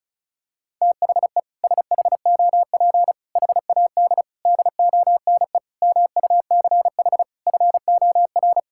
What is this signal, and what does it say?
Morse code, heard as a keyed tone, about 34 wpm, 700 Hz. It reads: THE SHOP HAD DONE MUCH FOR
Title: Sentences from Top 1600 Words 35wpm